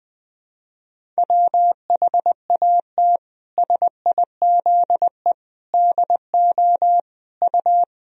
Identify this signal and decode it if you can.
Morse code, heard as a keyed tone, about 20 wpm, 700 Hz. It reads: WHAT SIZE DO U
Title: Sentences from Top 400 Words and Abbreviations 20wpm